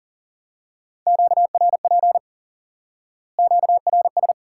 Morse code, heard as a keyed tone, about 37 wpm, 700 Hz. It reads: QRP QRS
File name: IQE Course 3 - Lesson 89 - YAGI - Two Elements - 40wpm